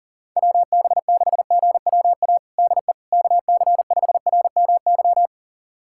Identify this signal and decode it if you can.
Morse code, heard as a keyed tone, about 40 wpm, 700 Hz. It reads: WB6GWA DE KC5RMY
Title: USA Callsign DE Callsign 40wpm